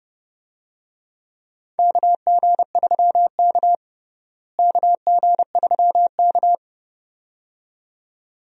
Morse code, heard as a keyed tone, about 30 wpm, 700 Hz. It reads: KG3K KG3K
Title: USA Call Signs - 4 Characters - Repeated 2x - 30wpm